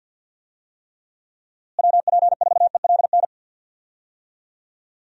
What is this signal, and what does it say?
Morse code, heard as a keyed tone, about 52 wpm, 700 Hz. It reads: WP4ELN
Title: USA Call Signs - 6 Characters - Repeated 3x - 50wpm